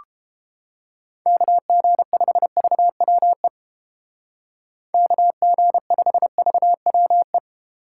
Morse code, heard as a keyed tone, about 28 wpm, 700 Hz. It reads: KG5VWE KG5VWE